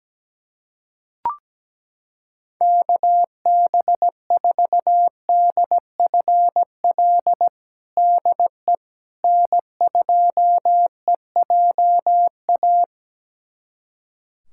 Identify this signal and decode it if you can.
Morse code, heard as a keyed tone, about 17 wpm, 700 Hz. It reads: KB4DFL DE N2EJA
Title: USA Callsign DE Callsign 17wpm